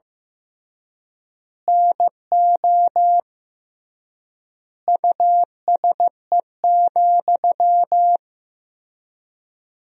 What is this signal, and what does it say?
Morse code, heard as a keyed tone, about 15 wpm, 700 Hz. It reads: NO USE,
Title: 3x Word-Spacing - Sentences from Top 500 Words - 15wpm